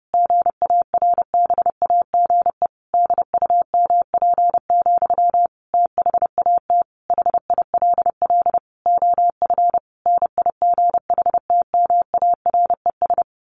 Morse code, heard as a keyed tone, about 30 wpm, 700 Hz. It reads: GARBAGE DUMP, THAT HILL OF NIGHTMARES
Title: A Sound Of Thunder 30wpm